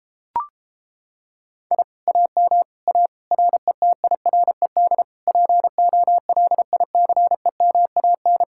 Morse code, heard as a keyed tone, about 29 wpm, 700 Hz. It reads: I AM A RETIRED POLICEMAN